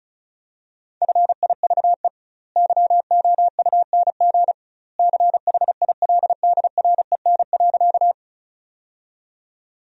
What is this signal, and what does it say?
Morse code, heard as a keyed tone, about 34 wpm, 700 Hz. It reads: FIVE YOUNG CHILDREN.
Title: 2x Word-Spacing - Sentences from Top 400 Words - 35wpm